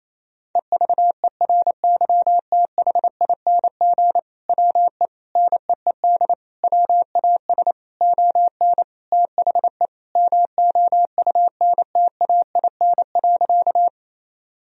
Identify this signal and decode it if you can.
Morse code, heard as a keyed tone, about 27 wpm, 700 Hz. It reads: EVERYTHING WE NEED WAS ON THE MOUNTAIN.
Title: Sentences from Top 1000 Words - Review Part 1 - 28wpm